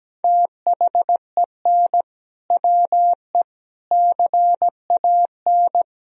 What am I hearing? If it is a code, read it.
Morse code, heard as a keyed tone, about 17 wpm, 700 Hz. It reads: THEN WE CAN
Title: Sentences from Top 1000 Words - Review Part 2 - 17wpm